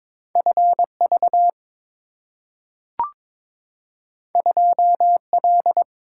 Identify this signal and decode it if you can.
Morse code, heard as a keyed tone, about 22 wpm, 700 Hz. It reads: FV 2L
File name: practice / 22wpm / Two Letter-Number Combinations - 22wpm